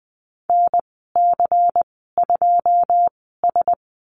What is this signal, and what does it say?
Morse code, heard as a keyed tone, about 20 wpm, spaced as for 15 wpm, 700 Hz. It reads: NC2S
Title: 186 Lesson 30g - Letter b - Call Signs 20wpm